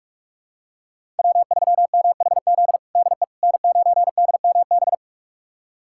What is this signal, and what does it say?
Morse code, heard as a keyed tone, about 45 wpm, 700 Hz. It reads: W3MHZ DE N9DMB
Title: USA Callsign DE Callsign - 5 Characters - 45wpm